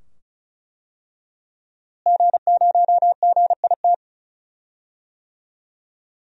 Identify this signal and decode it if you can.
Morse code, heard as a keyed tone, about 35 wpm, 700 Hz. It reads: G0GIT
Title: DXCC-100 Call Signs - 5 Characters - Repeated 2x - 35wpm